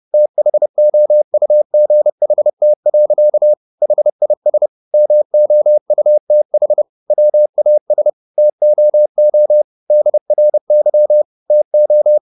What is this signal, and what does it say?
Morse code, heard as a keyed tone, about 30 wpm, 600 Hz. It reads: THOUGHT. HIS MOUTH WAS TOO DRY TO